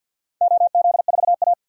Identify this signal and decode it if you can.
Morse code, heard as a keyed tone, about 51 wpm, 700 Hz. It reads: OZ4A
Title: DXCC-100 Call Signs - 4 Characters - 50wpm